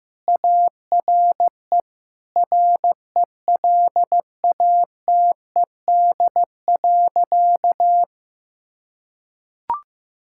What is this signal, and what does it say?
Morse code, heard as a keyed tone, about 15 wpm, 690 Hz. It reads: ARE RELATED.
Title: Sentences from Top 1000 Words - Review Part 5 - 15wpm